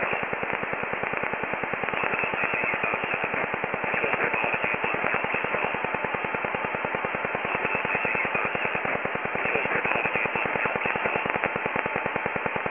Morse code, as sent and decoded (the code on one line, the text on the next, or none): none